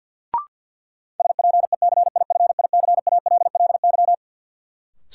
Morse code, heard as a keyed tone, about 49 wpm, 700 Hz. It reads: SPECIFICALLY
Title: Sentences from Top 1100 Words - New Word Drill Alt - 50wpm